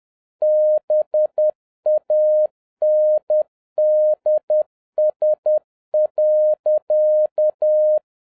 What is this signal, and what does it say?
Morse code, heard as a keyed tone, about 10 wpm, 610 Hz. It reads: BANDS.
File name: 2011-10-mp3 / n11-MEETING WITH GOOD FRIENDS 10wpm